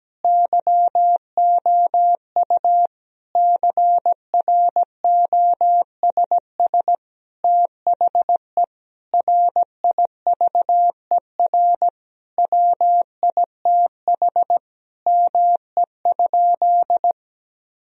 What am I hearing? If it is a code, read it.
Morse code, heard as a keyed tone, about 17 wpm, 700 Hz. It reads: YOU CROSS THE RIVER WITH ME?
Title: Sentences from Top 1000 Words - Review Part 3 - 17wpm